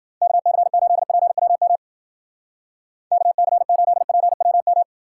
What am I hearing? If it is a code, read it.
Morse code, heard as a keyed tone, about 59 wpm, 700 Hz. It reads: KC8PWM KC8PWM